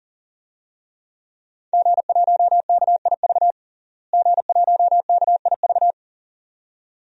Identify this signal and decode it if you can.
Morse code, heard as a keyed tone, about 40 wpm, 700 Hz. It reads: G1KIV G1KIV
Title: DXCC-100 Call Signs - 5 Characters - Repeated 2x - 40wpm